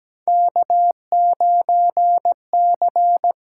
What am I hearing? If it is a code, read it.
Morse code, heard as a keyed tone, about 17 wpm, 700 Hz. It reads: K9C
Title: USA Call Signs - 5 Characters - 6x Character Spacing - 17wpm